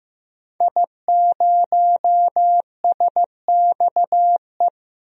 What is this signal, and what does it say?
Morse code, heard as a keyed tone, about 15 wpm, 700 Hz. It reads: I0SXE
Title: DXCC-100 Call Signs - 5 Characters - 15wpm